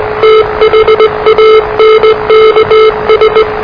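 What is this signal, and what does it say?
Morse code, heard as a keyed tone, about 20 wpm, 440 Hz, with noise. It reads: THANKS
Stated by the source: Morse challenge from NZ Net, the national amateur radio net of Aotearoa/New Zealand, which meets on 3535.0 kHz.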